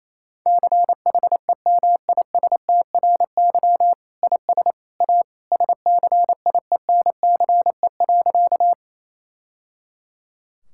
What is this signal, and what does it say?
Morse code, heard as a keyed tone, about 27 wpm, 700 Hz. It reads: CHEMISTRY IS A SCIENCE.
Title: Sentences from Top 1000 Words - Review Part 6 - 28wpm